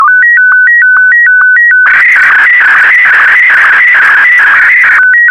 SSTV (slow-scan television), Martin 1.